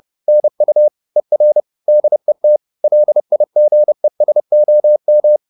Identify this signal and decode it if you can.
Morse code, heard as a keyed tone, about 30 wpm, 600 Hz. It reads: NU ER DET LIGESOM
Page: Ferie 30 wpm